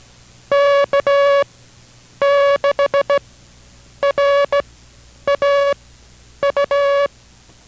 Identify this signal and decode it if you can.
Morse code, heard as a keyed tone, about 16 wpm, spaced as for 8 wpm, 560 Hz. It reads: K6RAU